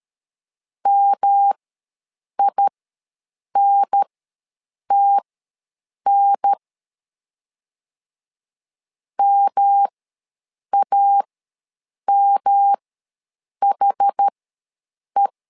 Morse code, heard as a keyed tone, about 13 wpm, spaced as for 7 wpm, 800 Hz. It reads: MINTN MAMHE
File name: LEZIONI / 3 MWRO / 3 M ws 12wpm 1745